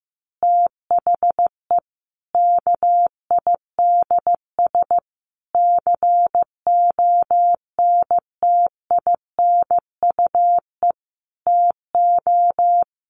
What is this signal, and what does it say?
Morse code, heard as a keyed tone, about 15 wpm, 700 Hz. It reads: THE KIDS CONTINUE TO
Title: Easy Phrases 15wpm